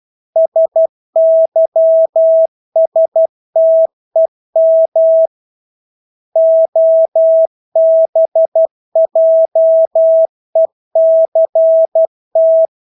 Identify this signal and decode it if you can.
Morse code, heard as a keyed tone, about 12 wpm, 650 Hz. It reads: SYSTEM OBJECT
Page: Engelske ord 12 wpm